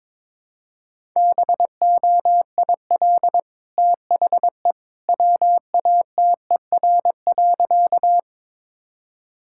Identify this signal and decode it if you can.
Morse code, heard as a keyed tone, about 21 wpm, 700 Hz. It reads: BOIL THE WATER.
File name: practice / 22wpm / Easy Sentences with Hints - 22wpm